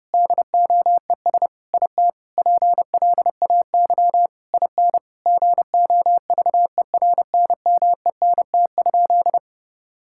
Morse code, heard as a keyed tone, about 32 wpm, 700 Hz. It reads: DOES IT PLAY IN GOVERNMENT?
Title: Sentences from Top 1000 Words - Review Part 6 - 30wpm